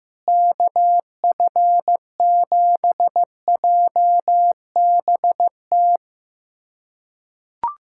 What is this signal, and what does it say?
Morse code, heard as a keyed tone, about 15 wpm, 700 Hz. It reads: KF7JBT
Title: USA Callsign DE Callsign 15wpm